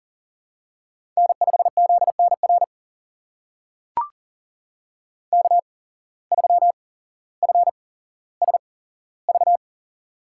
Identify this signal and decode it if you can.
Morse code, heard as a keyed tone, about 38 wpm, spaced as for 17 wpm, 700 Hz. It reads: N5ZNR K3FSV
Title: USA Call Signs - 5 Characters - 8x Character Spacing - 40wpm